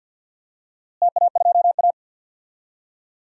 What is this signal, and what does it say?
Morse code, heard as a keyed tone, about 47 wpm, 700 Hz. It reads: TA2A
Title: DXCC-100 Call Signs - 4 Characters - Repeated 2x - 50wpm